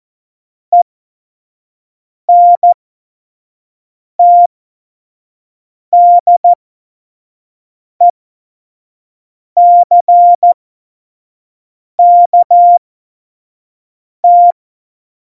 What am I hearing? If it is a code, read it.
Morse code, heard as a keyed tone, about 14 wpm, spaced as for 5 wpm, 700 Hz. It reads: ENTDECKT
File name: Morsekurs / Witze / Witz 26. Januar 2021, 25 F70